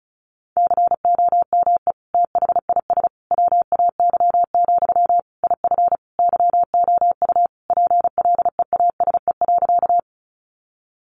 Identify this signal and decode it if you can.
Morse code, heard as a keyed tone, about 34 wpm, 700 Hz. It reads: COME THIS WAY, IF YOU PLEASE.
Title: Sentences from Top 1500 Words 35wpm